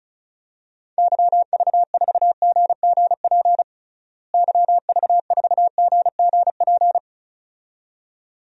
Morse code, heard as a keyed tone, about 34 wpm, 700 Hz. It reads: YV4GGP YV4GGP
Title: DXCC-100 Call Signs - 6 Characters - Repeated 2x - 35wpm